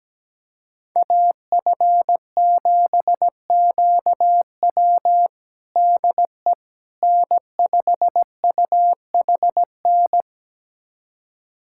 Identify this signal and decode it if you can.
Morse code, heard as a keyed tone, about 17 wpm, 700 Hz. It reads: AF7QW DE N5UHN
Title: USA Callsign DE Callsign - 5 Characters - 17wpm